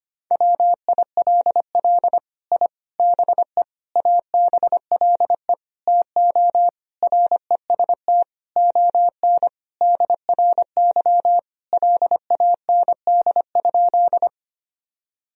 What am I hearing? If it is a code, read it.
Morse code, heard as a keyed tone, about 25 wpm, 700 Hz. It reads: WILL I BE ABLE TO REST ON DRY LAND?
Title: Sentences from Top 1000 Words - Review Part 3 - 25wpm